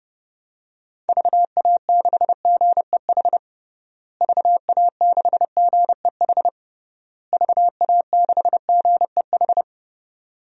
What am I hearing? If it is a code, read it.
Morse code, heard as a keyed tone, about 30 wpm, 700 Hz. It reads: VA6GEH VA6GEH VA6GEH
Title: DXCC-100 Call Signs - 6 Characters - Repeated 3x - 30wpm